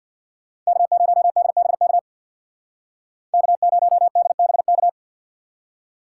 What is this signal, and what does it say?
Morse code, heard as a keyed tone, about 60 wpm, 700 Hz. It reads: K0DBK K0DBK
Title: Speed Racing - USA 1x3 Call Signs 50wpm